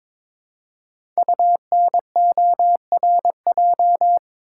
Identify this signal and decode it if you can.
Morse code, heard as a keyed tone, about 22 wpm, 700 Hz. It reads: UNORJ
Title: Code Groups - 5 Letters - 22wpm